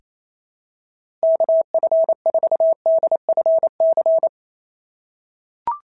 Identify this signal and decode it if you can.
Morse code, heard as a keyed tone, about 28 wpm, 650 Hz. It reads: KF4DFC
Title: SOTA QSO Practice 28wpm